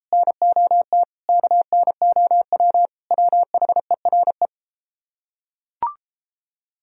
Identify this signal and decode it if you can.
Morse code, heard as a keyed tone, about 31 wpm, 700 Hz. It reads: NOT KNOW WHERE